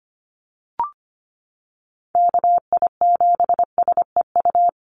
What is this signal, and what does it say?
Morse code, heard as a keyed tone, about 25 wpm, 700 Hz. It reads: KI7SEU
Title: USA Call Signs Repeated 3x 25wpm